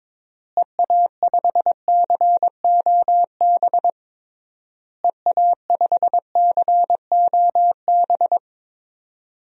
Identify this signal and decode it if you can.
Morse code, heard as a keyed tone, about 22 wpm, 700 Hz. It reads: EA5COB EA5COB
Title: DXCC-100 Call Signs - 6 Characters - Repeated 3x - 22wpm